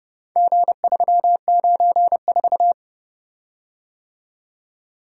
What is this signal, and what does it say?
Morse code, heard as a keyed tone, about 30 wpm, 700 Hz. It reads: G394